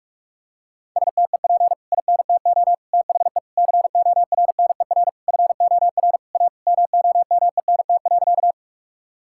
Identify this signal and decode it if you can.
Morse code, heard as a keyed tone, about 46 wpm, 700 Hz. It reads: STEP INTO THE CORNER FOR A MOMENT.
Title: Sentences from Top 1600 Words 45wpm